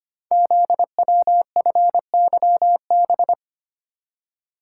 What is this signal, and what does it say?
Morse code, heard as a keyed tone, about 25 wpm, 700 Hz. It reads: ZWFYB
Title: Code Groups - 5 Letters - 25wpm